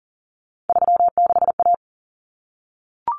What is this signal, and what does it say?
Morse code, heard as a keyed tone, about 40 wpm, 700 Hz. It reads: 36A
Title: Common Cut Numbers - 3-Digit 40wpm